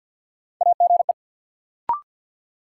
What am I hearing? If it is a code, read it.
Morse code, heard as a keyed tone, about 52 wpm, 700 Hz. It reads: AGE
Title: Sets of 3 Words - New Word Drill - 50wpm